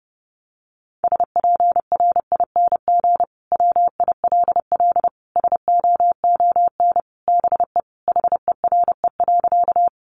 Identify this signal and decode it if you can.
Morse code, heard as a keyed tone, about 30 wpm, 700 Hz. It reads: SPRING WILL SOON BE HERE.
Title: Sentences from Top 1200 Words 30wpm